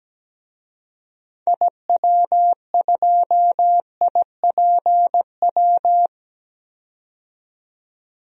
Morse code, heard as a keyed tone, about 17 wpm, 700 Hz. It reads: IW2IPW
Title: DXCC-100 Call Signs - 6 Characters - 17wpm